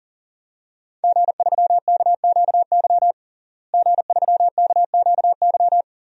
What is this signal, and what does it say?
Morse code, heard as a keyed tone, about 40 wpm, 700 Hz. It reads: G3KQY G3KQY
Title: DXCC-100 Call Signs - 5 Characters - Repeated 3x - 40wpm